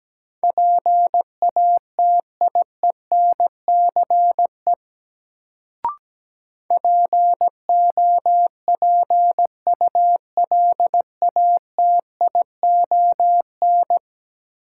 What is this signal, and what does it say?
Morse code, heard as a keyed tone, about 17 wpm, 700 Hz. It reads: PATIENCE POPULATION
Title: Sentences from Top 1300 Words - New Word Drill Alt - 17wpm